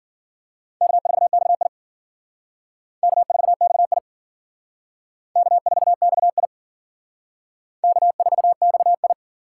Morse code, heard as a keyed tone, about 50 wpm, 700 Hz. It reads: K4XI K4XI K4XI K4XI
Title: Speed Racing - USA 1x2 Call Signs 40wpm